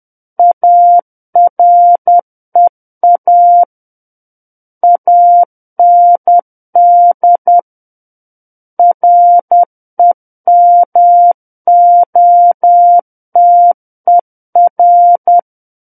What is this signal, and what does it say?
Morse code, heard as a keyed tone, about 10 wpm, 700 Hz. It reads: AREA AND REMOTER
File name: Book One / 01-Chapter 1 - 10 WPM